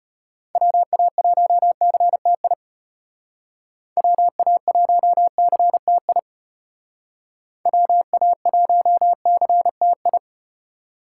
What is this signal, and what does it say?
Morse code, heard as a keyed tone, about 33 wpm, 700 Hz. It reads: WA1CTI WA1CTI WA1CTI